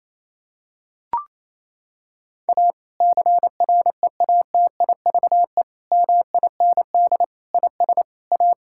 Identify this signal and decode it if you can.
Morse code, heard as a keyed tone, about 28 wpm, 700 Hz. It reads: A CREATIVE MIND IS A
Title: Sentences from Top 1000 Words - Review Part 5 - 28wpm